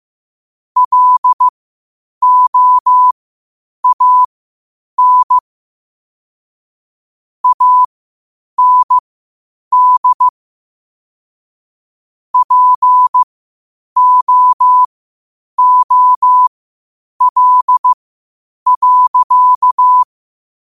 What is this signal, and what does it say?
Morse code, heard as a keyed tone, about 15 wpm, spaced as for 8 wpm, 1000 Hz. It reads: LOAN AND POOL.
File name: qotdfiles / 8wpm